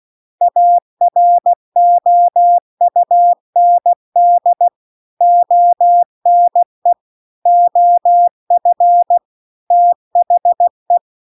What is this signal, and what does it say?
Morse code, heard as a keyed tone, about 16 wpm, 700 Hz. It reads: AROUND ONE OF THE
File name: CW-The-Big-Twoheaded-River-1-16WPM